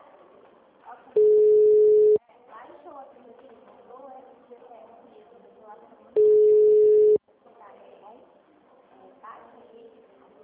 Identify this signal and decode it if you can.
Ringback tone (Europe, 425 Hz), with speech.
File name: out- -2010- - - .6405